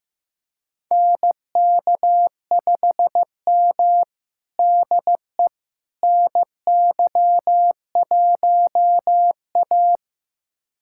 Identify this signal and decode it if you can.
Morse code, heard as a keyed tone, about 15 wpm, 700 Hz. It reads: NK5M DE NY1A
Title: USA Callsign DE Callsign - 4 Characters - 15wpm